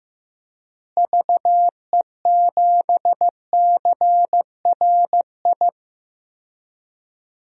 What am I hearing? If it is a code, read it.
Morse code, heard as a keyed tone, about 15 wpm, 700 Hz. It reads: VE7CRI
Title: DXCC-100 Call Signs - 6 Characters - Repeated 3x - 15wpm